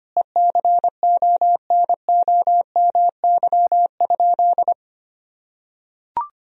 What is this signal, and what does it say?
Morse code, heard as a keyed tone, about 25 wpm, 700 Hz. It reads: ECONOMY?
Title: Sentences from Top 1000 Words - Review Part 3 - 25wpm